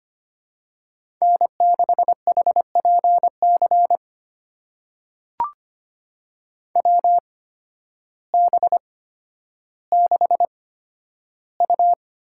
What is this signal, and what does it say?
Morse code, heard as a keyed tone, about 25 wpm, spaced as for 12 wpm, 700 Hz. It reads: N6HPC WB6U
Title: USA Call Signs - 8x Character Spacing - 25wpm